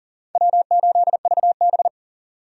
Morse code, heard as a keyed tone, about 40 wpm, 700 Hz. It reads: W8VB
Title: USA Call Signs - 4 Characters - Repeated 3x - 40wpm